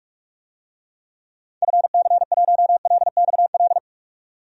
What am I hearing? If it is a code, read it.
Morse code, heard as a keyed tone, about 42 wpm, 700 Hz. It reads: FC1LXL
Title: DXCC-100 Call Signs - 6 Characters - 45wpm